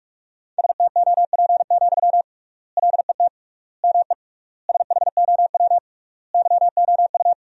Morse code, heard as a keyed tone, about 46 wpm, 700 Hz. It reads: STOP, LET ME SHOW YOU
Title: 3x Word-Spacing - Sentences from Top 300 Words - 45wpm